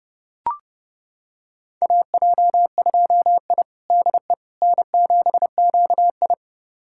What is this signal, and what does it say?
Morse code, heard as a keyed tone, about 30 wpm, 700 Hz. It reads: AJ2I DE N7QI
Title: USA Callsign DE Callsign - 4 Characters - 30wpm